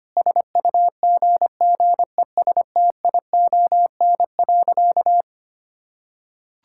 Morse code, heard as a keyed tone, about 25 wpm, 700 Hz. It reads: SUGGESTION.
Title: Sentences from Top 1600 Words 25wpm